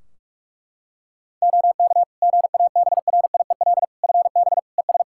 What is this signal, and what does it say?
Morse code, heard as a keyed tone, about 41 wpm, 700 Hz. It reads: OK GABRIEL FB ES
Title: Practice QSO - Exchange 2 - Info - 45wpm